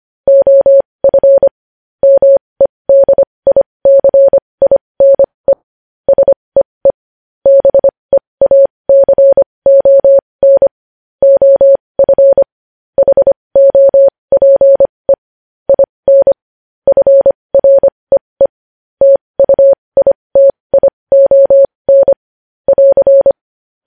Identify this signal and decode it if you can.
Morse code, heard as a keyed tone, about 25 wpm, spaced as for 20 wpm, 550 Hz. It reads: OF MEDICINE SEE BEACON OF HOPE IN FREE TUITION <AR>
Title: News Headlines at 20